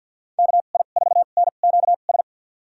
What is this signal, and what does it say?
Morse code, heard as a keyed tone, about 51 wpm, spaced as for 36 wpm, 700 Hz. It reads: KI4NQS
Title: USA Call Signs - 2x Character Spacing - 50wpm